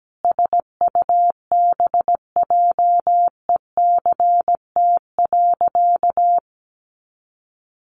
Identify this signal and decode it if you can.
Morse code, heard as a keyed tone, about 17 wpm, 700 Hz. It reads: SUBJECT.
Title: Sentences from Top 800 Words 17wpm